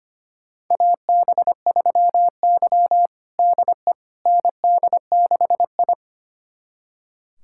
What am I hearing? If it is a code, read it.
Morse code, heard as a keyed tone, about 25 wpm, 700 Hz. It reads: AB3Y DE ND6I
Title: USA Callsign DE Callsign - 4 Characters - 25wpm